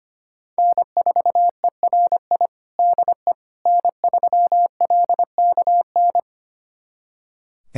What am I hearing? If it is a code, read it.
Morse code, heard as a keyed tone, about 25 wpm, 700 Hz. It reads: N4ERI DE N3LKN
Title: USA Callsign DE Callsign - 5 Characters - 25wpm